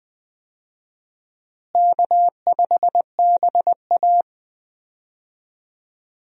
Morse code, heard as a keyed tone, about 20 wpm, 700 Hz. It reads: K5BA
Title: USA Call Signs - 4 Characters - Repeated 2x - 20wpm